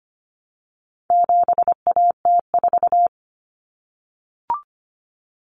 Morse code, heard as a keyed tone, about 25 wpm, 700 Hz. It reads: 7AT4
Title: Common Cut Numbers - 4-Digit 25wpm